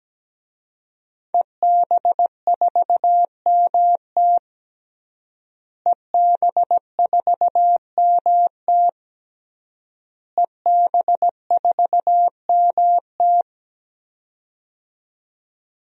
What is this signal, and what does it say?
Morse code, heard as a keyed tone, about 17 wpm, 700 Hz. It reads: EB4MT EB4MT EB4MT
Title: DXCC-100 Call Signs - 5 Characters - Repeated 3x - 17wpm